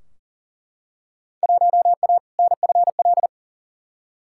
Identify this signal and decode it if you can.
Morse code, heard as a keyed tone, about 40 wpm, 700 Hz. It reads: 1A NFL
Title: Field Day Exchanges Encore - 40wpm